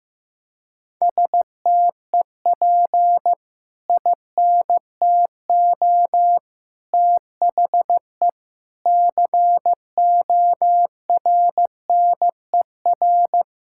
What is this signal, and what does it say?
Morse code, heard as a keyed tone, about 15 wpm, 700 Hz. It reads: STEP INTO THE CORNER
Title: Sentences from Top 1600 Words 15wpm